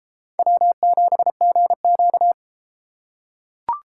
Morse code, heard as a keyed tone, about 30 wpm, 700 Hz. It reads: W7GQ